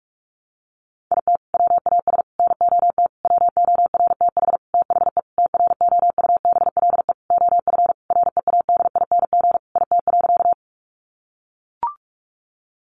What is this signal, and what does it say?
Morse code, heard as a keyed tone, about 44 wpm, 700 Hz. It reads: IT WAS NOT WORTH THE TROUBLE OF READING IT.
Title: Sentences from Top 1200 Words 45wpm